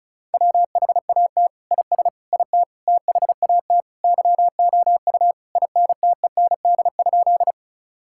Morse code, heard as a keyed tone, about 34 wpm, 700 Hz. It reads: WHAT IS IT THAT YOU INTEND?
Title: Sentences from Top 1600 Words 35wpm